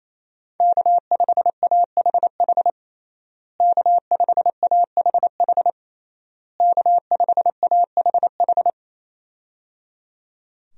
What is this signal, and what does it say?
Morse code, heard as a keyed tone, about 28 wpm, 700 Hz. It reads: K5AHH K5AHH K5AHH
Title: USA Call Signs - 5 Characters - Repeated 3x - 28wpm